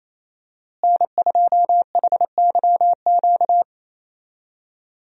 Morse code, heard as a keyed tone, about 28 wpm, 700 Hz. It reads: N2HYQ
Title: USA Call Signs - 5 Characters - Repeated 2x - 28wpm